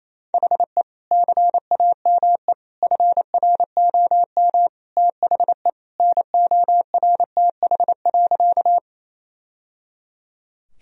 Morse code, heard as a keyed tone, about 28 wpm, 700 Hz. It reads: HE CAME FROM THE NORTH.
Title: Sentences from Top 1000 Words - Review Part 1 - 28wpm